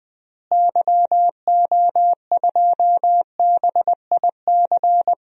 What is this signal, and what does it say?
Morse code, heard as a keyed tone, about 21 wpm, 700 Hz. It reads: YO2BIC
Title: DXCC-100 Call Signs - 6 Characters - Repeated 3x - 20wpm